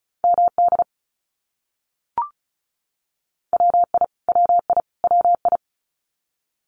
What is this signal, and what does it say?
Morse code, heard as a keyed tone, about 34 wpm, 700 Hz. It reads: MD WI WI WI
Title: US States Repeated 3x 35wpm